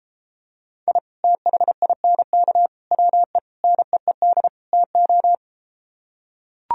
Morse code, heard as a keyed tone, about 34 wpm, 700 Hz. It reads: I THINK WE NEED TO